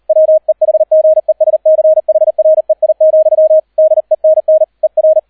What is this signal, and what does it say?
Morse code, heard as a keyed tone, about 38 wpm, 630 Hz, with radio noise. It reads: WEHGESCHREI, DENN ER